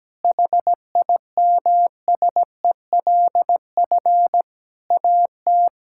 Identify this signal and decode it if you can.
Morse code, heard as a keyed tone, about 17 wpm, 700 Hz. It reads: HIMSELF AT
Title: Sentences from Top 1600 Words 17wpm